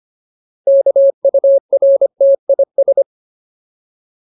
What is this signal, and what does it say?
Morse code, heard as a keyed tone, about 25 wpm, 550 Hz. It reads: KURTIS
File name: Ultimate Word List / 25WPM Ultimate